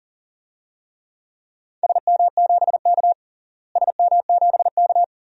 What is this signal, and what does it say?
Morse code, heard as a keyed tone, about 40 wpm, 700 Hz. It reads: SM7K SM7K
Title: DXCC-100 Call Signs - 4 Characters - Repeated 2x - 40wpm